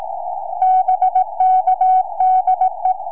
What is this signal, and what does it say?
Morse code, heard as a keyed tone, about 18 wpm, 750 Hz. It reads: BK DE
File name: sm4bdq 2m